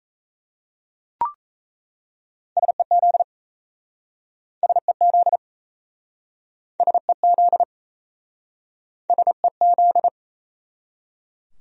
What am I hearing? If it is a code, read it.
Morse code, heard as a keyed tone, about 37 wpm, 700 Hz. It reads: SEZ SEZ SEZ SEZ